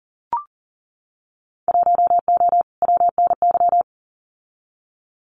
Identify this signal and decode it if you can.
Morse code, heard as a keyed tone, about 41 wpm, 700 Hz. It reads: 1O WNY
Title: Winter Field Day Exchanges 40wpm